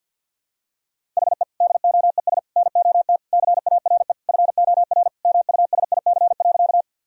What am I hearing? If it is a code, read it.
Morse code, heard as a keyed tone, about 51 wpm, 700 Hz. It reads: HE DOES NOT CARE FOR MUSIC.
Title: Sentences from Top 1000 Words - Review Part 1 - 50wpm